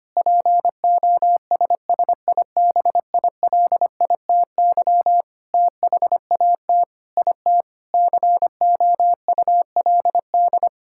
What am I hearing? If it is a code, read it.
Morse code, heard as a keyed tone, about 25 wpm, 700 Hz. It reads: POSSIBILITY THAT IT COULD
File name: practice / 25wpm / Sentences with 9-Word Phrases - 25wpm